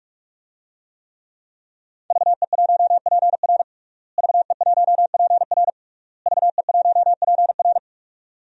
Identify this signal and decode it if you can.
Morse code, heard as a keyed tone, about 47 wpm, 700 Hz. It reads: VE1PR VE1PR VE1PR
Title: DXCC-100 Call Signs - 5 Characters - Repeated 3x - 45wpm